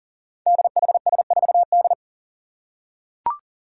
Morse code, heard as a keyed tone, about 40 wpm, 700 Hz. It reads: DHS4D
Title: Five Challenging Letter-Number Combinations 40wpm